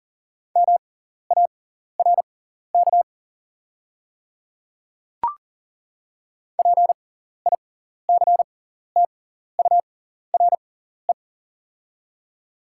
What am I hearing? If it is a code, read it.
Morse code, heard as a keyed tone, about 40 wpm, spaced as for 14 wpm, 690 Hz. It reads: MARK PICTURE
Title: 6x Character Spacing - Top 300 Words - 40wpm